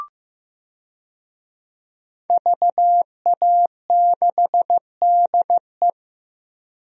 Morse code, heard as a keyed tone, about 15 wpm, 700 Hz. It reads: VA6DE
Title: DXCC-100 Call Signs - 5 Characters - 15wpm